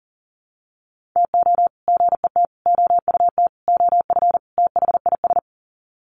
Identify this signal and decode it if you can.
Morse code, heard as a keyed tone, about 40 wpm, 700 Hz. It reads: TO GET OUT OF THIS
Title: Sets of 5 Words Encore2 40wpm